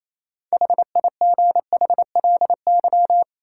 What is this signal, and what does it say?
Morse code, heard as a keyed tone, about 28 wpm, 700 Hz. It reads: HIGHLY
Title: Sentences from Top 1100 Words - New Word Drill Alt - 28wpm